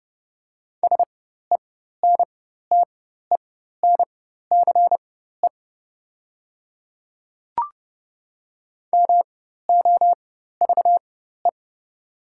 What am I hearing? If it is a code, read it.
Morse code, heard as a keyed tone, about 30 wpm, spaced as for 14 wpm, 700 Hz. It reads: SENTENCE MOVE
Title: 4x Character Spacing - Top 300 Words - 30wpm